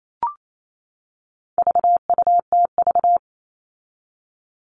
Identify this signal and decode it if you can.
Morse code, heard as a keyed tone, about 28 wpm, 700 Hz. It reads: VUTV
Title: All Cut Numbers - 4-Digit 28wpm